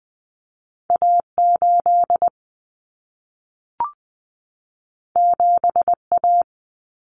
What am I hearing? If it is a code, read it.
Morse code, heard as a keyed tone, about 20 wpm, 700 Hz. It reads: A8 7A
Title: Common Cut Numbers - 2-Digit 20wpm